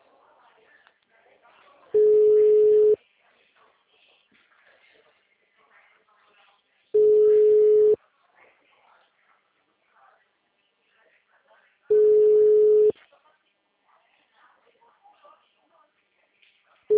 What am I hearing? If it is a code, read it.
Ringback tone (Europe, 425 Hz), with speech.